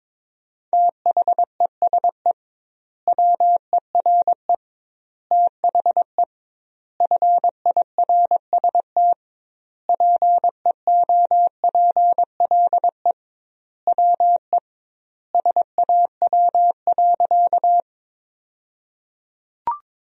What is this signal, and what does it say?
Morse code, heard as a keyed tone, about 22 wpm, 700 Hz. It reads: THESE WERE THE FIRST PEOPLE WE SAW.
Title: 2x Word-Spacing - Sentences from Top 300 Words - 22wpm